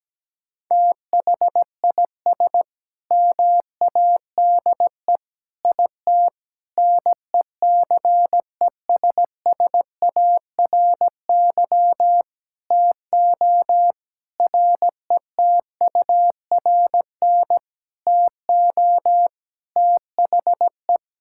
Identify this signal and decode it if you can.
Morse code, heard as a keyed tone, about 17 wpm, 700 Hz. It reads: THIS MADE IT NECESSARY TO RETURN TO THE
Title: Sentences from Top 1600 Words 17wpm